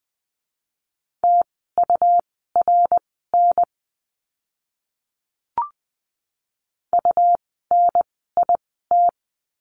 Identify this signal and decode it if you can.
Morse code, heard as a keyed tone, about 20 wpm, spaced as for 14 wpm, 700 Hz. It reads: TURN UNIT
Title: 101 Lesson 17f - Letter u - Words 20wpm